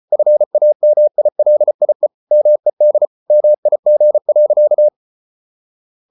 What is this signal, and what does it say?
Morse code, heard as a keyed tone, about 34 wpm, 600 Hz. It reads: FAMILIE MED MIG.
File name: Ferie QRQ 34wpm